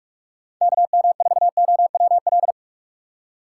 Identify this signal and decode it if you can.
Morse code, heard as a keyed tone, about 49 wpm, 700 Hz. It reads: KM4OWL